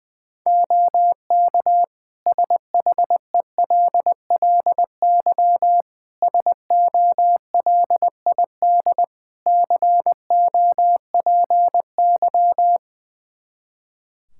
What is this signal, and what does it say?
Morse code, heard as a keyed tone, about 20 wpm, 700 Hz. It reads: OK SHELLY SOLID COPY
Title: Practice QSO - Exchange 3 - More Info - 20wpm